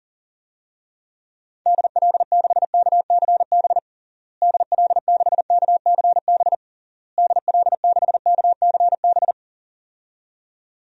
Morse code, heard as a keyed tone, about 40 wpm, 700 Hz. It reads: DL6KCB DL6KCB DL6KCB
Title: DXCC-100 Call Signs - 6 Characters - Repeated 3x - 40wpm